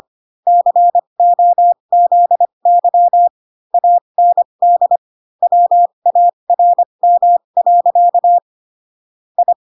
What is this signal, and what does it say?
Morse code, heard as a keyed tone, about 25 wpm, spaced as for 22 wpm, 700 Hz. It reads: COZY AND WARM. I